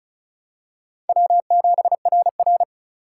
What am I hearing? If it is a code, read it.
Morse code, heard as a keyed tone, about 36 wpm, 700 Hz. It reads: W7RR
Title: USA Call Signs - 4 Characters - 4x Character Spacing - 35wpm